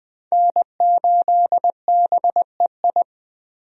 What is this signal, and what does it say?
Morse code, heard as a keyed tone, about 20 wpm, 700 Hz. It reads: N8BEI
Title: USA Call Signs - 5 Characters - Repeated 3x - 20wpm